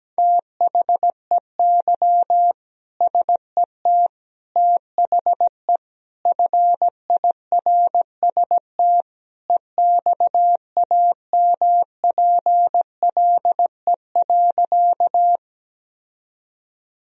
Morse code, heard as a keyed tone, about 17 wpm, 700 Hz. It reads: THEY SET THE FIRST EXAMPLE.
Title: Sentences from Top 1000 Words - Review Part 1 - 17wpm